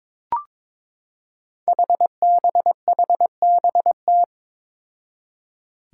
Morse code, heard as a keyed tone, about 22 wpm, 700 Hz. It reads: HBHBT
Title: Five Challenging Letter-Number Combinations 22wpm